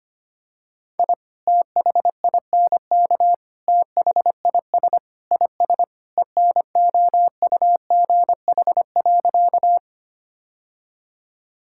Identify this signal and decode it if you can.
Morse code, heard as a keyed tone, about 25 wpm, 700 Hz. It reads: I THINK THIS IS ENOUGH.
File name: practice / 25wpm / Sentences with 4-Word Phrases - 25wpm